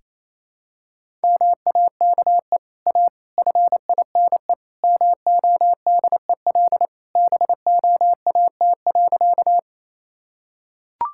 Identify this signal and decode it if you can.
Morse code, heard as a keyed tone, about 28 wpm, 700 Hz. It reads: MAKE A FINE MODEL BOAT.
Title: Sentences from Top 1000 Words - Review Part 5 - 28wpm